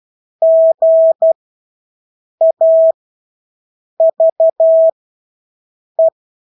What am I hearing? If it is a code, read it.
Morse code, heard as a keyed tone, about 12 wpm, spaced as for 6 wpm, 650 Hz. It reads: GAVE
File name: Engelske ord 12-6 wpm